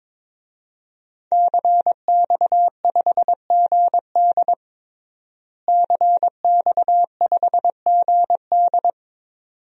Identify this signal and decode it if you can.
Morse code, heard as a keyed tone, about 22 wpm, 700 Hz. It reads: CX5GD CX5GD
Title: DXCC-100 Call Signs - 5 Characters - Repeated 2x - 22wpm